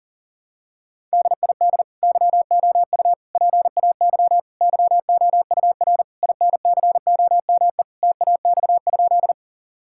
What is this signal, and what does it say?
Morse code, heard as a keyed tone, about 40 wpm, 700 Hz. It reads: DID YOU PAY YOUR INCOME TAX?
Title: Sentences from Top 1000 Words - Review Part 5 - 40wpm